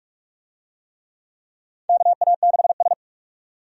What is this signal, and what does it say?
Morse code, heard as a keyed tone, about 41 wpm, 700 Hz. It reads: KA6S
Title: USA Call Signs - 4 Characters - 45wpm